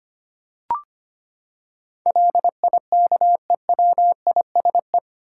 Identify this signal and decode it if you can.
Morse code, heard as a keyed tone, about 25 wpm, 700 Hz. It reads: LIKEWISE
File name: Eight Letter Words - 25wpm